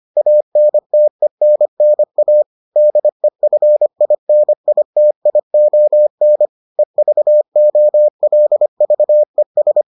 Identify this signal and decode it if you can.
Morse code, heard as a keyed tone, about 25 wpm, 600 Hz. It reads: ANTENNA DEFINITION EVOLVES